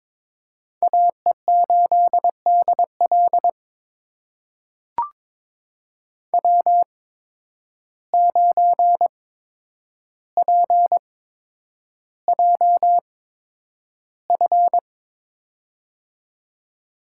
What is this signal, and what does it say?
Morse code, heard as a keyed tone, about 22 wpm, spaced as for 9 wpm, 700 Hz. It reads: AE8DL W9PJF
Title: USA Call Signs - 5 Characters - 8x Character Spacing - 22wpm